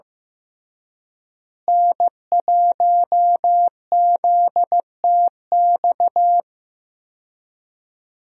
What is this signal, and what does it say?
Morse code, heard as a keyed tone, about 16 wpm, 700 Hz. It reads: N1ZTX
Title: USA Call Signs - 5 Characters - Repeated 3x - 15wpm